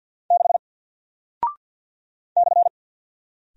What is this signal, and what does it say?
Morse code, heard as a keyed tone, about 50 wpm, 700 Hz. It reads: 6 /
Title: Single Character (Letter:Number:Punctuation:Prosign) Alt 50wpm